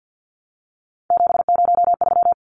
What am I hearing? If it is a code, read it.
Morse code, heard as a keyed tone, about 50 wpm, 700 Hz. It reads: 703
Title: Numbers - 3-Digit 50wpm